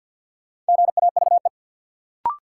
Morse code, heard as a keyed tone, about 52 wpm, 700 Hz. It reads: GAVE
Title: Sentences from Top 500 Words - New Word Drill - 50wpm